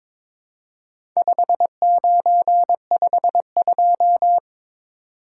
Morse code, heard as a keyed tone, about 23 wpm, 700 Hz. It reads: 5952
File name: practice / 22wpm / Four Letter-Number Combinations - 22wpm